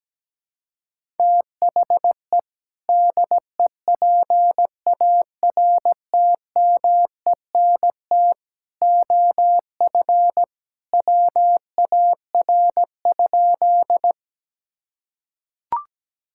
Morse code, heard as a keyed tone, about 17 wpm, 700 Hz. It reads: THE DEPARTMENT OF WAR?
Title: Sentences from Top 1000 Words - Review Part 5 - 17wpm